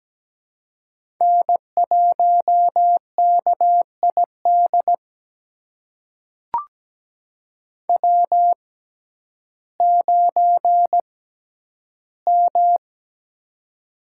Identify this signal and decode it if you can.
Morse code, heard as a keyed tone, about 17 wpm, spaced as for 10 wpm, 700 Hz. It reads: N1KID W9M
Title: USA Call Signs - 5 Characters - 6x Character Spacing - 17wpm